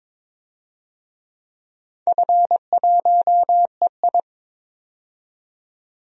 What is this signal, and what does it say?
Morse code, heard as a keyed tone, about 22 wpm, 700 Hz. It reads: F1EI
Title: DXCC-100 Call Signs - 4 Characters - 22wpm